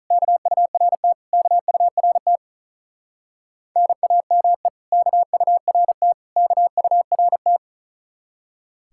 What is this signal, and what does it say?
Morse code, heard as a keyed tone, about 44 wpm, 700 Hz. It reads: KURT KURT NAME KURT KURT